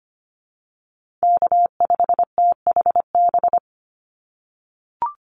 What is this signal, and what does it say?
Morse code, heard as a keyed tone, about 25 wpm, 700 Hz. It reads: K5THB
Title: USA Call Signs 25wpm